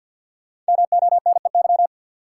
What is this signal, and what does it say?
Morse code, heard as a keyed tone, about 46 wpm, 700 Hz. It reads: MONEY
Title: Sets of 4 Words - New Word Drill - 50wpm